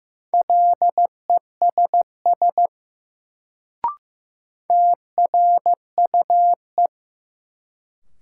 Morse code, heard as a keyed tone, about 15 wpm, 700 Hz. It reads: LESS TRUE
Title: Sentences from Top 400 Words - New Word Drill - 15wpm